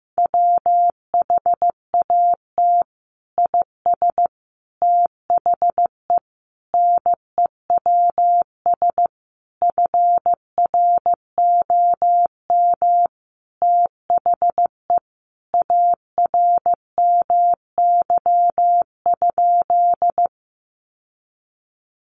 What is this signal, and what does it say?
Morse code, heard as a keyed tone, about 15 wpm, 700 Hz. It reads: WHAT IS THE NEWS FROM THE ARMY?
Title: Sentences from Top 800 Words 15wpm